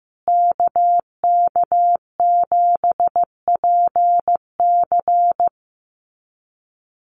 Morse code, heard as a keyed tone, about 15 wpm, 700 Hz. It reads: KK7PC
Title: USA Call Signs Repeated 3x 15wpm